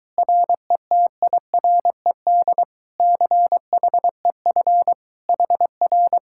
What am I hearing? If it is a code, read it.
Morse code, heard as a keyed tone, about 24 wpm, 700 Hz. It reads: RETIRED CHEF HR